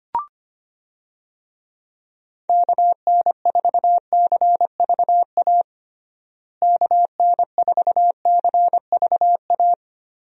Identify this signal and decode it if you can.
Morse code, heard as a keyed tone, about 25 wpm, 700 Hz. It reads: KN4CVA KN4CVA
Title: USA Call Signs - 6 Characters - Repeated 2x - 25wpm